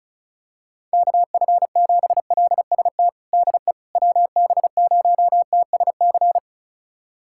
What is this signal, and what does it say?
Morse code, heard as a keyed tone, about 36 wpm, 700 Hz. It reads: KF7LST DE WB0TSC
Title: USA Callsign DE Callsign 35wpm